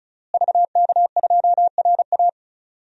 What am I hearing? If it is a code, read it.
Morse code, heard as a keyed tone, about 36 wpm, 700 Hz. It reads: VK2RA
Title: DXCC-100 Call Signs - 5 Characters - Repeated 3x - 35wpm